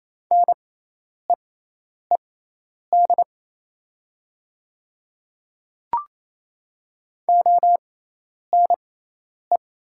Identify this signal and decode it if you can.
Morse code, heard as a keyed tone, about 27 wpm, spaced as for 10 wpm, 700 Hz. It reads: NEED ONE
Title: 6x Character Spacing - Top 200 Words - 28wpm